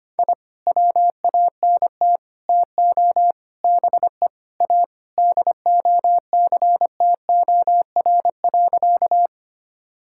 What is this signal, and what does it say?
Morse code, heard as a keyed tone, about 25 wpm, 700 Hz. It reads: I WANT TO BE A DOCTOR.
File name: practice / 25wpm / Sentences with 3-Word Phrases - 25wpm